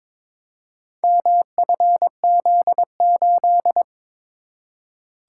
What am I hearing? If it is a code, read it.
Morse code, heard as a keyed tone, about 22 wpm, 700 Hz. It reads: MFZ8
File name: practice / 22wpm / Four Letter-Number Combinations - 22wpm